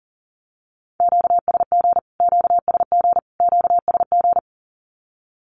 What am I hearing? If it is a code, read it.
Morse code, heard as a keyed tone, about 40 wpm, 700 Hz. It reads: QSG QSG QSG
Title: Q-Codes Repeated 3x 40wpm